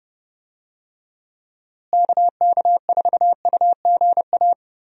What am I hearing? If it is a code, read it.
Morse code, heard as a keyed tone, about 32 wpm, 700 Hz. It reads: KK4UGA
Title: USA Call Signs - 6 Characters - Repeated 3x - 30wpm